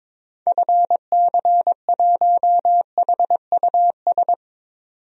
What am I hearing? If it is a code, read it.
Morse code, heard as a keyed tone, about 22 wpm, 700 Hz. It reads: FC1HUS
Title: DXCC-100 Call Signs - 6 Characters - 22wpm